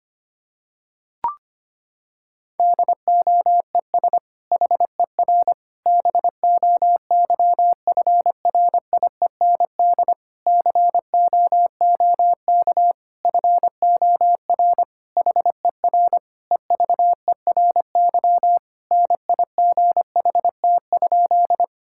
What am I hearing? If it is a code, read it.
Morse code, heard as a keyed tone, about 25 wpm, 700 Hz. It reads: DOES HER BOYFRIEND COOK FOR HER EVERY NIGHT?
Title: Sentences from Top 1000 Words - Review Part 5 - 25wpm